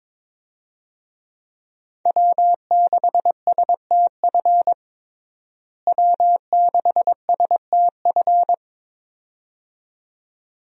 Morse code, heard as a keyed tone, about 22 wpm, 700 Hz. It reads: W6STF W6STF
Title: USA Call Signs - 5 Characters - Repeated 2x - 22wpm